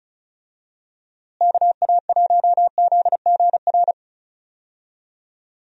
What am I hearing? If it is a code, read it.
Morse code, heard as a keyed tone, about 34 wpm, 700 Hz. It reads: KA1ZGR
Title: USA Call Signs - 6 Characters - Repeated 3x - 35wpm